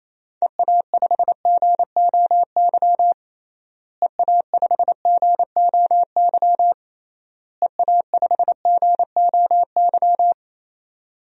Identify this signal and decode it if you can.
Morse code, heard as a keyed tone, about 28 wpm, 700 Hz. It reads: EA5GOY EA5GOY EA5GOY
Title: DXCC-100 Call Signs - 6 Characters - Repeated 3x - 28wpm